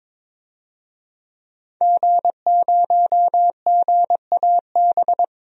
Morse code, heard as a keyed tone, about 22 wpm, 700 Hz. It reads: G0GAB
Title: DXCC-100 Call Signs - 5 Characters - 22wpm